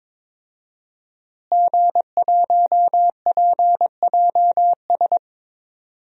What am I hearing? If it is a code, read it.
Morse code, heard as a keyed tone, about 22 wpm, 700 Hz. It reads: G1PJS
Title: DXCC-100 Call Signs - 5 Characters - Repeated 3x - 22wpm